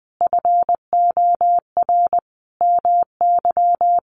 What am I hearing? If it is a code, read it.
Morse code, heard as a keyed tone, about 20 wpm, 700 Hz. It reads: FOR MY
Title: Sentences from Top 1200 Words 20wpm